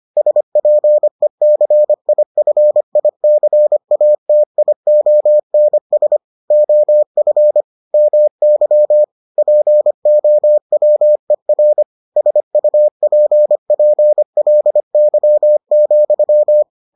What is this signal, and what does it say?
Morse code, heard as a keyed tone, about 25 wpm, 600 Hz. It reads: SPECIFICATIONS OF MY POWER SUPPLY,